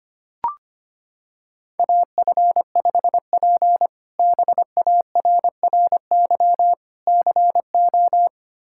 Morse code, heard as a keyed tone, about 25 wpm, 700 Hz. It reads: AF5P BARRY CO
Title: K1USN SST QSO Practice - 25wpm